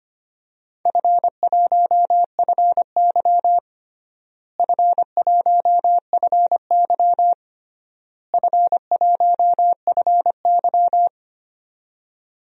Morse code, heard as a keyed tone, about 25 wpm, 700 Hz. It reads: F1FY F1FY F1FY
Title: DXCC-100 Call Signs - 4 Characters - Repeated 3x - 25wpm